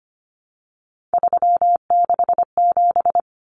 Morse code, heard as a keyed tone, about 25 wpm, 700 Hz. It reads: 367
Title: Common Cut Numbers - 3-Digit 25wpm